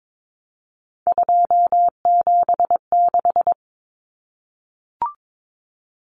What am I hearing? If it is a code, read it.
Morse code, heard as a keyed tone, about 22 wpm, 700 Hz. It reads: 276
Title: Common Cut Numbers - 3-Digit 22wpm